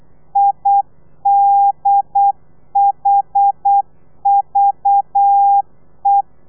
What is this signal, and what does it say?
Morse code, heard as a keyed tone, about 8 wpm, 800 Hz. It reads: IDHVE
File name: telegrafia / i2tfj / Lez4 35